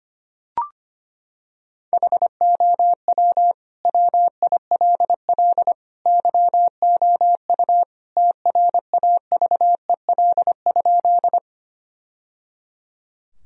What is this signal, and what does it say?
Morse code, heard as a keyed tone, about 25 wpm, 700 Hz. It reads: HOW WILL YOU TRAVEL?
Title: Sentences from Top 1000 Words - Review Part 2 - 25wpm